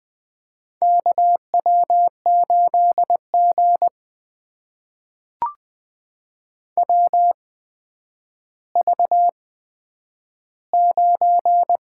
Morse code, heard as a keyed tone, about 20 wpm, spaced as for 10 wpm, 700 Hz. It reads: KW8G WV9
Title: USA Call Signs - 4 Characters - 8x Character Spacing - 20wpm